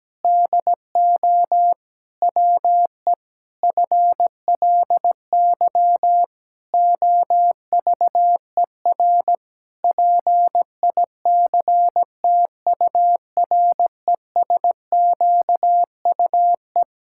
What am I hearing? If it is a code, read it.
Morse code, heard as a keyed tone, about 17 wpm, 700 Hz. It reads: DO WE FLY OVER PICTURESQUE
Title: Pangram Challenge v2 - 17wpm